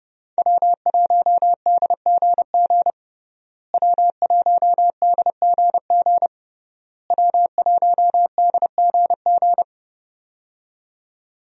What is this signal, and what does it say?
Morse code, heard as a keyed tone, about 30 wpm, 700 Hz. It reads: W1DGG W1DGG W1DGG
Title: USA Call Signs - 5 Characters - Repeated 3x - 30wpm